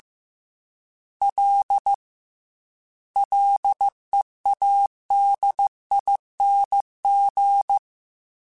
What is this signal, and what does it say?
Morse code, heard as a keyed tone, about 15 wpm, 800 Hz. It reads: L LEADING
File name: Ls / Single L m3s / Single L @13